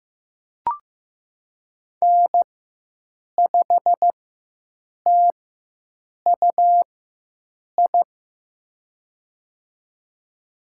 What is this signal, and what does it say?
Morse code, heard as a keyed tone, about 15 wpm, spaced as for 7 wpm, 700 Hz. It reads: N5TUI
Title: USA Call Signs - 4x Character Spacing - 15wpm